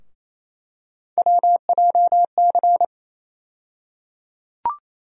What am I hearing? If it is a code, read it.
Morse code, heard as a keyed tone, about 28 wpm, 700 Hz. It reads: WJC
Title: USA Call Sign Postfixes - 3 Letters 28wpm